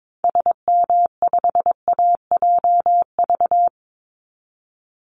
Morse code, heard as a keyed tone, about 22 wpm, 700 Hz. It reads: SM5AJV
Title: Call Signs Repeated 3x - 22wpm